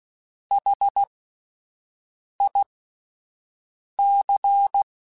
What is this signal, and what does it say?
Morse code, heard as a keyed tone, about 16 wpm, spaced as for 5 wpm, 800 Hz. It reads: HIC
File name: Alphabetical / Alphabetical mp3s / Hs @5